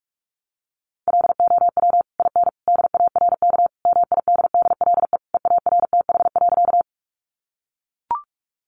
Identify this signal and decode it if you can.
Morse code, heard as a keyed tone, about 44 wpm, 700 Hz. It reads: LOW IN DARK MIDDLE EARTH.
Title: Sentences from Top 900 Words 45wpm